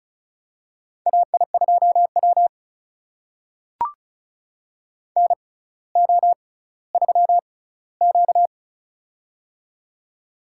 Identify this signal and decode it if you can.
Morse code, heard as a keyed tone, about 36 wpm, spaced as for 18 wpm, 700 Hz. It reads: AI2W NO3Q
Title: USA Call Signs - 4 Characters - 6x Character Spacing - 35wpm